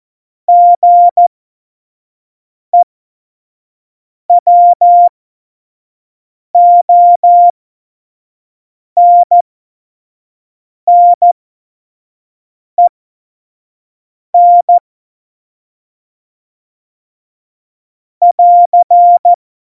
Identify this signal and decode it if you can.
Morse code, heard as a keyed tone, about 14 wpm, spaced as for 5 wpm, 700 Hz. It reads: GEWONNEN <AR>
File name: Morsekurs / Witze / Witz 15. September 2020, 25 F70